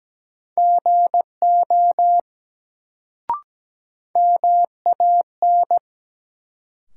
Morse code, heard as a keyed tone, about 17 wpm, 700 Hz. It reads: GO MAN
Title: Sets of 2 Words - New Word Drill - 17wpm